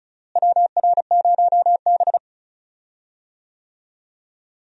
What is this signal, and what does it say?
Morse code, heard as a keyed tone, about 36 wpm, 700 Hz. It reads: WR0B
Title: USA Call Signs - 4 Characters - Repeated 2x - 35wpm